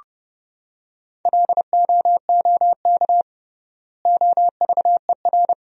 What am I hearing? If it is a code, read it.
Morse code, heard as a keyed tone, about 30 wpm, 700 Hz. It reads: LOOK OVER
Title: 3x Word-Spacing - Sentences from Top 100 Words - 30wpm